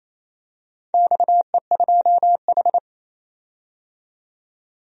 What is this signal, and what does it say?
Morse code, heard as a keyed tone, about 27 wpm, 700 Hz. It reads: XE2H
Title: DXCC-100 Call Signs - 4 Characters - 28wpm